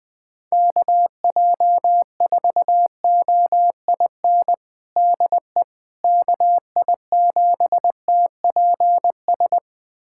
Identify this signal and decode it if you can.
Morse code, heard as a keyed tone, about 20 wpm, 700 Hz. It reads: KJ4OIN DE KI7TPS
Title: USA Callsign DE Callsign 20wpm